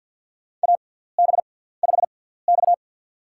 Morse code, heard as a keyed tone, about 47 wpm, spaced as for 17 wpm, 700 Hz. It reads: AB5X
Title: USA Call Signs - 4 Characters - 6x Character Spacing - 50wpm